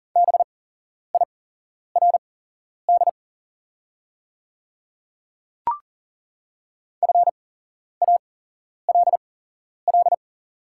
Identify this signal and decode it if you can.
Morse code, heard as a keyed tone, about 40 wpm, spaced as for 11 wpm, 700 Hz. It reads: BIRD FALL
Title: 8x Character Spacing - Top 500 Words - 40wpm